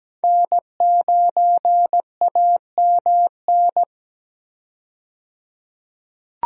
Morse code, heard as a keyed tone, about 17 wpm, 700 Hz. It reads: N9AMN
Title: USA Call Signs - 5 Characters - 17wpm